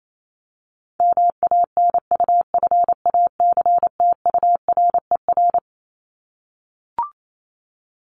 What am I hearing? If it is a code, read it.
Morse code, heard as a keyed tone, about 28 wpm, 700 Hz. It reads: MANUFACTURER
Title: Top 1751-2000 Words 28wpm